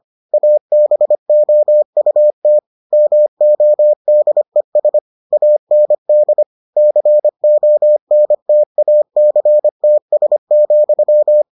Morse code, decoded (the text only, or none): ABOUT MODES AND CONTACTS,